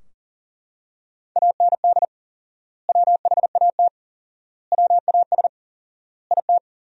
Morse code, heard as a keyed tone, about 40 wpm, 700 Hz. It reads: AND WHAT WAS IT
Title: 4x Word-Spacing - Sentences from Top 100 Words - 40wpm